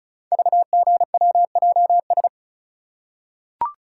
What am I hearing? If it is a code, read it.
Morse code, heard as a keyed tone, about 36 wpm, 690 Hz. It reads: VGWJS
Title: Code Groups - 5 Letters - 35wpm